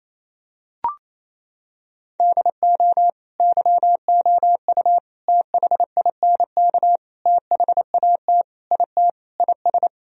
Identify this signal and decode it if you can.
Morse code, heard as a keyed tone, about 28 wpm, 700 Hz. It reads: DO YOU THINK THAT IT IS
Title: Sentences from Top 1000 Words - Review Part 4 - 28wpm